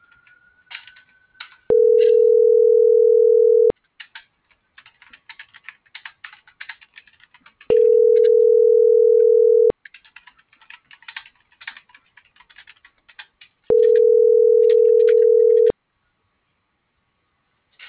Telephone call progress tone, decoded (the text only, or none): ringback tone